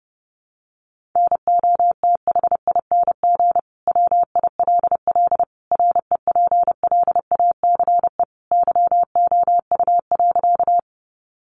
Morse code, heard as a keyed tone, about 28 wpm, 700 Hz. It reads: NOTHING WILL REPLACE YOU.
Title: Sentences from Top 1400 Words 30wpm